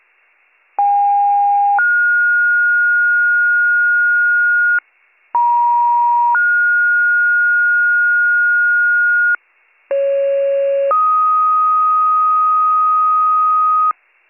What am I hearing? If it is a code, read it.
Two-tone page, heard as a one-second tone, then a three-second tone, about 810 then 1430 Hz, about 950 then 1430 Hz, about 550 then 1250 Hz, with noise. It is heard from 0.8 to 13.9 s.